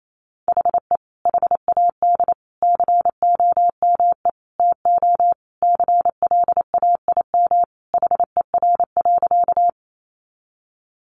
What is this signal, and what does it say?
Morse code, heard as a keyed tone, about 28 wpm, 700 Hz. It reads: HE HAD COME TO CLAIM HER.
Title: Sentences from Top 1500 Words 28wpm